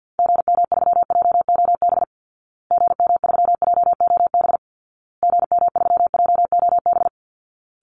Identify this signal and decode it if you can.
Morse code, heard as a keyed tone, about 50 wpm, 700 Hz. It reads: GM3JOB GM3JOB GM3JOB
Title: Call Signs Repeated 2x - 50wpm